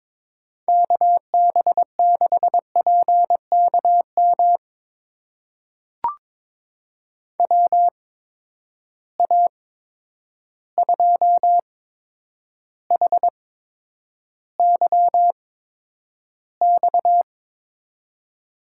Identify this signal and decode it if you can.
Morse code, heard as a keyed tone, about 22 wpm, spaced as for 9 wpm, 700 Hz. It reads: KB6PKM WA2HYX
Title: USA Call Signs - 6 Characters - 8x Character Spacing - 22wpm